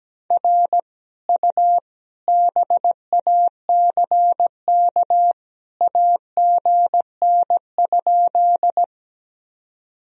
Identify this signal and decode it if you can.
Morse code, heard as a keyed tone, about 17 wpm, 700 Hz. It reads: R U BACK AGN?
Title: Sentences from Top 200 Words and Abbreviations 17wpm